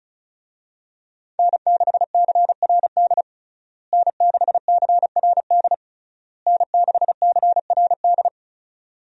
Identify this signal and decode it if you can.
Morse code, heard as a keyed tone, about 34 wpm, 700 Hz. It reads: N6CRD N6CRD N6CRD
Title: USA Call Signs - 5 Characters - Repeated 3x - 35wpm